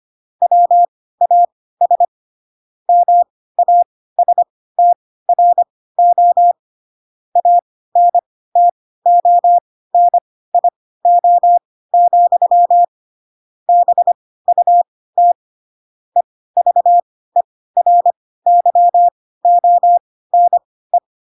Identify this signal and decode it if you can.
Morse code, heard as a keyed tone, about 25 wpm, spaced as for 16 wpm, 700 Hz. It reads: WAS MASTRO ANTONIO, BUT EVERYONE
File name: Pinocchio / Pinocchio ch1 10 25 ik1hge